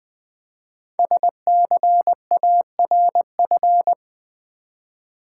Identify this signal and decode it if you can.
Morse code, heard as a keyed tone, about 20 wpm, 700 Hz. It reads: SCARF
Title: Easy Words with Hints - 20wpm